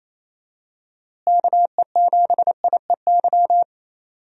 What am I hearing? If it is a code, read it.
Morse code, heard as a keyed tone, about 28 wpm, 700 Hz. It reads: KE7IEY
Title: USA Call Signs - 6 Characters - Repeated 3x - 28wpm